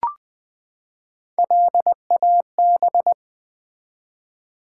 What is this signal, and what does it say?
Morse code, heard as a keyed tone, about 20 wpm, 700 Hz. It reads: LAB
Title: Build A Word - 3 Letters At A Time - 6-Letter Words - 20wpm